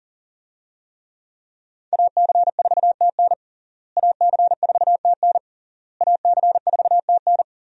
Morse code, heard as a keyed tone, about 40 wpm, 700 Hz. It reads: AC4TN AC4TN AC4TN
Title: USA Call Signs - 5 Characters - Repeated 3x - 40wpm